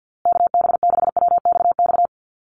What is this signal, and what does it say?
Morse code, heard as a keyed tone, about 50 wpm, 700 Hz. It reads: KB6WXX
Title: USA Call Signs Repeated 2x 50wpm